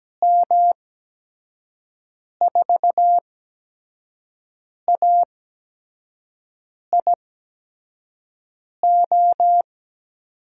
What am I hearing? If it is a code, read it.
Morse code, heard as a keyed tone, about 17 wpm, spaced as for 5 wpm, 700 Hz. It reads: M4AIO
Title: USA Call Signs - 6 Characters - 8x Character Spacing - 17wpm